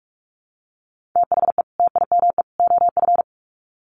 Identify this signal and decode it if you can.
Morse code, heard as a keyed tone, about 44 wpm, 700 Hz. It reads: THE TIME OF
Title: Sets of 3 Words 45wpm